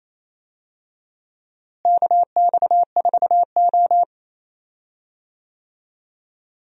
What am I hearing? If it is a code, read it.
Morse code, heard as a keyed tone, about 28 wpm, 700 Hz. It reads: KX4O
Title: USA Call Signs - 4 Characters - Repeated 2x - 28wpm